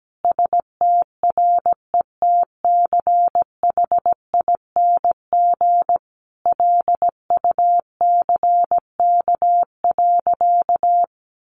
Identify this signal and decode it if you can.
Morse code, heard as a keyed tone, about 17 wpm, 700 Hz. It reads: STRETCHING LUCK.
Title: A Sound Of Thunder 17wpm